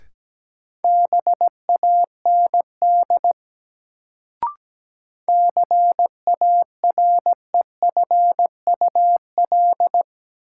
Morse code, heard as a keyed tone, about 17 wpm, 700 Hz. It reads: BAND CAREFUL
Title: Sentences from Top 1600 Words - New Word Drill - 17wpm